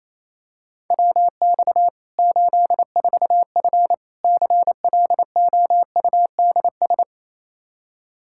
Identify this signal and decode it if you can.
Morse code, heard as a keyed tone, about 28 wpm, 700 Hz. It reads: WX 84F CLOUDS
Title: Practice QSO - Exchange 2 - Info - 28wpm